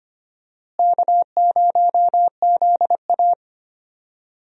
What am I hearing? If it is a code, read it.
Morse code, heard as a keyed tone, about 25 wpm, 700 Hz. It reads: K0ZA
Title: USA Call Signs - 4 Characters - Repeated 2x - 25wpm